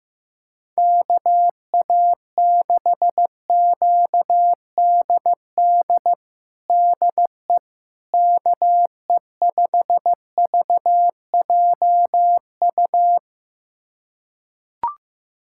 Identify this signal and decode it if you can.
Morse code, heard as a keyed tone, about 15 wpm, 700 Hz. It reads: KA6QDD DE KE5VJU
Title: USA Callsign DE Callsign 15wpm